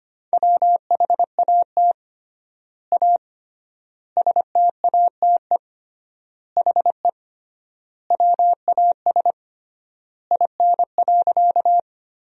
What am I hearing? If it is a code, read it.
Morse code, heard as a keyed tone, about 25 wpm, 700 Hz. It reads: WHAT A STATE HE WAS IN.
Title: 3x Word-Spacing - Sentences from Top 400 Words - 25wpm